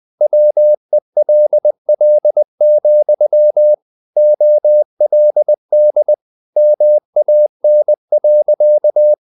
Morse code, heard as a keyed tone, about 20 wpm, 600 Hz. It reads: WELL, OLD MAN.
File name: the old man and the sea part 2 20wpm